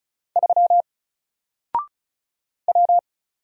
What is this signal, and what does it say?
Morse code, heard as a keyed tone, about 34 wpm, 700 Hz. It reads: 3 W
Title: Single Letter-Number 35wpm